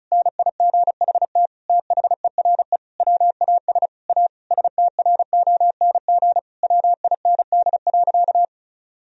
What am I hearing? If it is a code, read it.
Morse code, heard as a keyed tone, about 36 wpm, 700 Hz. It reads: NIGHT THERE WAS A STRONG WIND.
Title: Sentences from Top 1000 Words - Review Part 4 - 35wpm